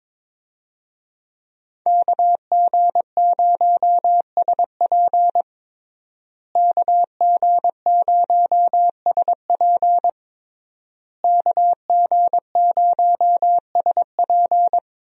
Morse code, heard as a keyed tone, about 21 wpm, 700 Hz. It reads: KG0SP KG0SP KG0SP
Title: USA Call Signs - 5 Characters - Repeated 3x - 22wpm